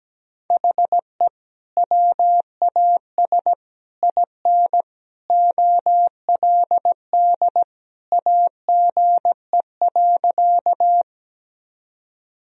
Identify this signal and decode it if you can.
Morse code, heard as a keyed tone, about 17 wpm, 700 Hz. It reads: HE WAS IN OLD AGE.
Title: Sentences from Top 1000 Words - Review Part 3 - 17wpm